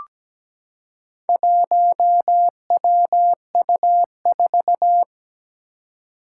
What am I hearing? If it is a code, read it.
Morse code, heard as a keyed tone, about 17 wpm, 700 Hz. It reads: 1WU4
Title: Four Challenging Letter-Number Combinations 17wpm